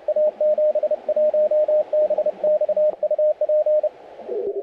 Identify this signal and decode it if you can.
Morse code, heard as a keyed tone, about 31 wpm, 610 Hz, with radio noise. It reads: A71BX UP
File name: audio mp3 files / A71BX 40m CW 2008-4-12 2-0-53